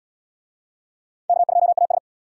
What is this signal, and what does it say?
Morse code, heard as a keyed tone, about 66 wpm, 700 Hz. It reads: K2AS